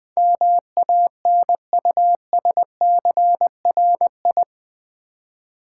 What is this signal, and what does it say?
Morse code, heard as a keyed tone, about 21 wpm, 700 Hz. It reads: MANUSCRI
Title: Build A Word - 4 Letters At A Time - 10-Letter Words - 20wpm